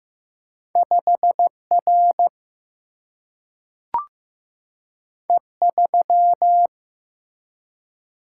Morse code, heard as a keyed tone, about 15 wpm, 700 Hz. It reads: 5R E3
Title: Single-Letter Single-Number 15wpm